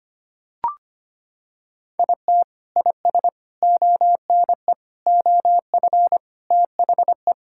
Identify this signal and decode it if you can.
Morse code, heard as a keyed tone, about 25 wpm, 700 Hz. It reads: IT IS ONE OF THE
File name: practice / 25wpm / Sentences with 4-Word Phrases - 25wpm